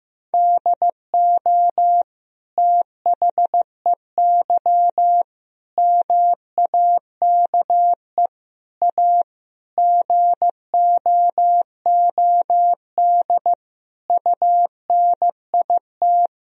Morse code, heard as a keyed tone, about 15 wpm, 700 Hz. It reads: DO THEY MAKE A GOOD UNIT
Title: Sentences from Top 1000 Words - Review Part 2 - 15wpm